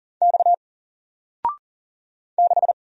Morse code, heard as a keyed tone, about 39 wpm, 700 Hz. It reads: X 6
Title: Single Character (Letter:Number:Punctuation:Prosign) 40wpm